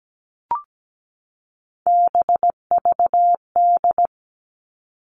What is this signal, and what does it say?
Morse code, heard as a keyed tone, about 17 wpm, 700 Hz. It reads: BVD
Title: Three Challenging Letter-Number Combinations 17wpm